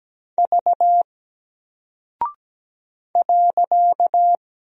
Morse code, heard as a keyed tone, about 17 wpm, 700 Hz. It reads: V .
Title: Single Character (Letter:Number:Punctuation:Prosign) Alt 17wpm